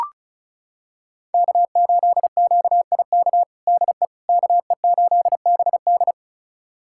Morse code, heard as a keyed tone, about 36 wpm, 700 Hz. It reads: K8QIK DE KE8BD
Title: USA Callsign DE Callsign - 5 Characters - 35wpm